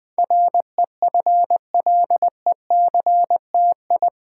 Morse code, heard as a keyed tone, about 20 wpm, 700 Hz. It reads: REFLECTI
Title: Build A Word - 4 Letters At A Time - 10-Letter Words - 20wpm